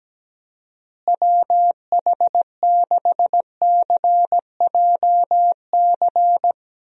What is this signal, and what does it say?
Morse code, heard as a keyed tone, about 17 wpm, 700 Hz. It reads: WH6CJC
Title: USA Call Signs - 6 Characters - Repeated 3x - 17wpm